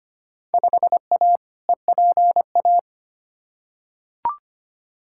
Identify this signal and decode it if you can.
Morse code, heard as a keyed tone, about 25 wpm, 700 Hz. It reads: 5A EPA
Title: Field Day Exchanges - 25wpm